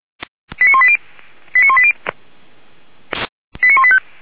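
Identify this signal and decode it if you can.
Five-tone selective call (ZVEI), heard as quick runs of five tones, tones of about 70 ms. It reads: 89180 89180 89185
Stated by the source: ZVEI five-tone selective call received on SDR.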